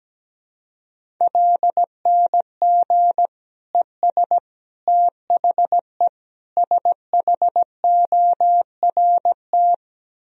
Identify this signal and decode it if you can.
Morse code, heard as a keyed tone, about 17 wpm, 700 Hz. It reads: LNG ES THE SHORT
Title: Sentences from Top 400 Words and Abbreviations 17wpm